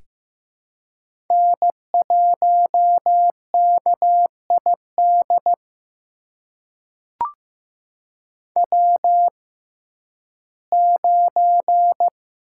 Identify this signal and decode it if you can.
Morse code, heard as a keyed tone, about 15 wpm, spaced as for 11 wpm, 700 Hz. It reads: N1KID W9
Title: USA Call Signs - 5 Characters - 6x Character Spacing - 15wpm